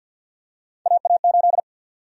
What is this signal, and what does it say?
Morse code, heard as a keyed tone, about 48 wpm, 700 Hz. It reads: AA8
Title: USA Call Sign Prefixes - Repeated 2x - 50wpm